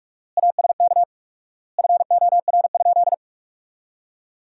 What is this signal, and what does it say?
Morse code, heard as a keyed tone, about 46 wpm, 700 Hz. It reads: ASK FOR?
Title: 4x Word-Spacing - Sentences from Top 200 Words - 45wpm